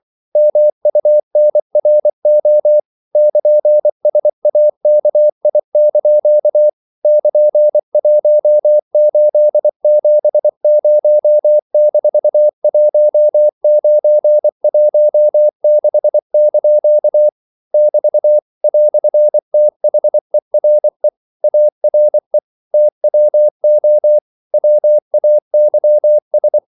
Morse code, heard as a keyed tone, about 24 wpm, 600 Hz. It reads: MUNRO <KN>SAKI) <KN>1870-1916) <BT> "THERE ARE TWO WAYS
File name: CW / Practice / EnglishQuotations0224